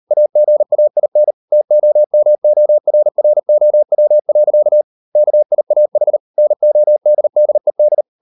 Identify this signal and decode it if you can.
Morse code, heard as a keyed tone, about 38 wpm, 600 Hz. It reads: AGAIN TOMORROW. KIAH NODDED